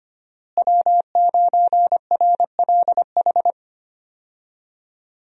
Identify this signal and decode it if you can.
Morse code, heard as a keyed tone, about 25 wpm, 700 Hz. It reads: W9RLH
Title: USA Call Signs - 5 Characters - Repeated 2x - 25wpm